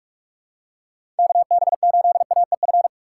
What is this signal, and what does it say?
Morse code, heard as a keyed tone, about 46 wpm, 700 Hz. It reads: KB8AEF
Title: USA Call Signs - 3x Character Spacing - 45wpm